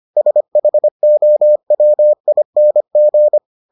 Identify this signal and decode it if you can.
Morse code, heard as a keyed tone, about 25 wpm, 600 Hz. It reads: SHOWING